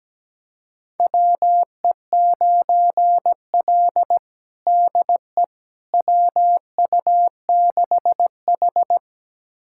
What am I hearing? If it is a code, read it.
Morse code, heard as a keyed tone, about 17 wpm, 700 Hz. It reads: WE9L DE WU6H
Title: USA Callsign DE Callsign - 4 Characters - 17wpm